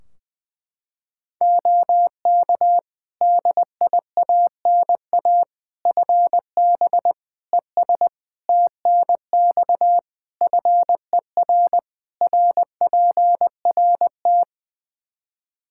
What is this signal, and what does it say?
Morse code, heard as a keyed tone, about 20 wpm, 700 Hz. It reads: OK DIANA FB ES TNX FER RPRT
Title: Practice QSO - Exchange 2 - Info - 20wpm